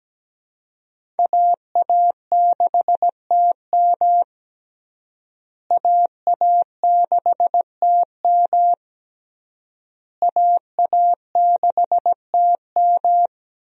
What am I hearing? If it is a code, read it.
Morse code, heard as a keyed tone, about 17 wpm, 700 Hz. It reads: AA6TM AA6TM AA6TM
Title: USA Call Signs - 5 Characters - Repeated 3x - 17wpm